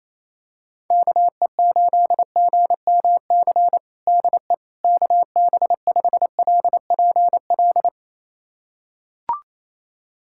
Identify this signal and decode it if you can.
Morse code, heard as a keyed tone, about 27 wpm, 700 Hz. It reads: KE8GMC DE KB5LPL
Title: USA Callsign DE Callsign - 6 Characters - 28wpm